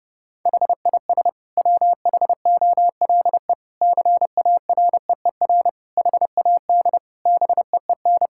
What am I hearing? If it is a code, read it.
Morse code, heard as a keyed tone, about 30 wpm, 700 Hz. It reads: HIS WHOLE CAREER HAD BEEN
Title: Sentences from Top 1000 Words - Review Part 6 - 30wpm